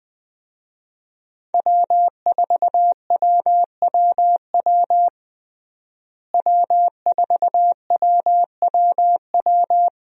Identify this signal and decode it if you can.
Morse code, heard as a keyed tone, about 20 wpm, 700 Hz. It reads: W4WWW W4WWW
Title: USA Call Signs - 5 Characters - Repeated 2x - 20wpm